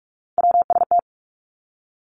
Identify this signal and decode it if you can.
Morse code, heard as a keyed tone, about 46 wpm, 700 Hz. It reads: WST
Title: Three Challenging Letter-Number Combinations 45wpm